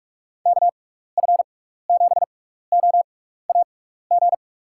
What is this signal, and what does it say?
Morse code, heard as a keyed tone, about 46 wpm, spaced as for 15 wpm, 700 Hz. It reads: KF7OAG
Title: USA Call Signs - 6 Characters - 6x Character Spacing - 45wpm